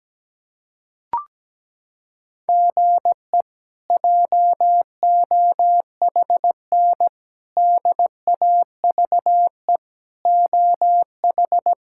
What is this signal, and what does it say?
Morse code, heard as a keyed tone, about 17 wpm, 700 Hz. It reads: GE JOHN DAVE OH
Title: K1USN SST QSO Practice - 17wpm